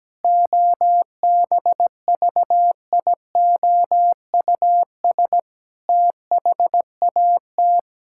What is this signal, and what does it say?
Morse code, heard as a keyed tone, about 17 wpm, 700 Hz. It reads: OBVIOUS THAT
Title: Sentences from Top 1600 Words 17wpm